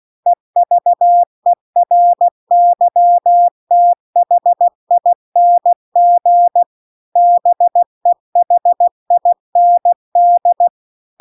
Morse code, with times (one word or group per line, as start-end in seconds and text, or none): 0.3-6.6 EVERYTHING
7.2-10.7 BEHIND